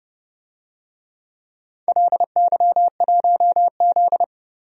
Morse code, heard as a keyed tone, about 30 wpm, 700 Hz. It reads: LY1Z
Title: DXCC-100 Call Signs - 4 Characters - Repeated 3x - 30wpm